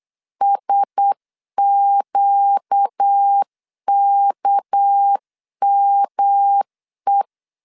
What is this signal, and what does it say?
Morse code, heard as a keyed tone, about 8 wpm, 800 Hz. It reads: SQKME
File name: CW / DET / 1 9 1500C S A 8wpm 5637